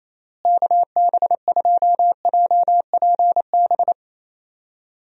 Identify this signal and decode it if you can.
Morse code, heard as a keyed tone, about 28 wpm, 700 Hz. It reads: KB2JPB
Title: USA Call Signs - 6 Characters - Repeated 3x - 28wpm